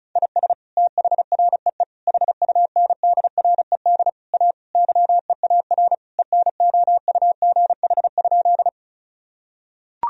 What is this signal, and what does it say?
Morse code, heard as a keyed tone, about 34 wpm, 700 Hz. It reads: IS THREE HUNDRED A YEAR ENOUGH?
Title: Sentences from Top 1000 Words - Review Part 2 - 35wpm